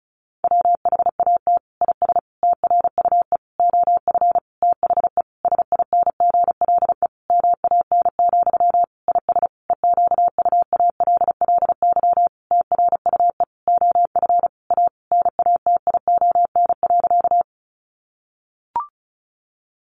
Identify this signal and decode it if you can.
Morse code, heard as a keyed tone, about 34 wpm, 700 Hz. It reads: WHAT IS TRUE OF THE SINGLE MAN, IS EQUALLY TRUE OF A NATION.
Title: Sentences from Top 1400 Words 35wpm